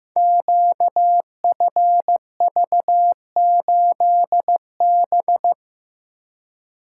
Morse code, heard as a keyed tone, about 15 wpm, 700 Hz. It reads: QFV8B
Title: Code Groups - 5 Letter:Numbers - 15wpm